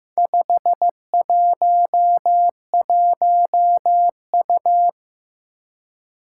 Morse code, heard as a keyed tone, about 15 wpm, 700 Hz. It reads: 511U
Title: Four Challenging Letter-Number Combinations 15wpm